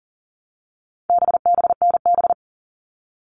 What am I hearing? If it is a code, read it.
Morse code, heard as a keyed tone, about 40 wpm, 700 Hz. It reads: BBNB
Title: All Cut Numbers - 4-Digit 40wpm